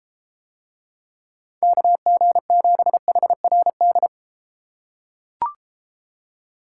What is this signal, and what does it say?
Morse code, heard as a keyed tone, about 32 wpm, 700 Hz. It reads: KG7HRD